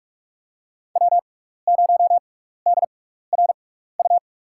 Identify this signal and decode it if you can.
Morse code, heard as a keyed tone, about 44 wpm, spaced as for 15 wpm, 700 Hz. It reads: W0DRU
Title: USA Call Signs - 5 Characters - 6x Character Spacing - 45wpm